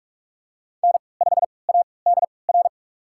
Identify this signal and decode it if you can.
Morse code, heard as a keyed tone, about 46 wpm, spaced as for 26 wpm, 700 Hz. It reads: N5ADR
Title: USA Call Signs - 5 Characters - 3x Character Spacing - 45wpm